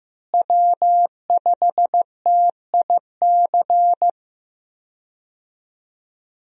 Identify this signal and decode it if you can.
Morse code, heard as a keyed tone, about 15 wpm, 700 Hz. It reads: W5TIC
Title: USA Call Signs - 5 Characters - Repeated 2x - 15wpm